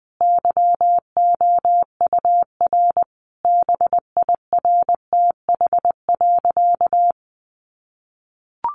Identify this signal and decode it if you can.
Morse code, heard as a keyed tone, about 20 wpm, 700 Hz. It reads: YOUR BIRTH.
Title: Sentences from Top 1300 Words 20wpm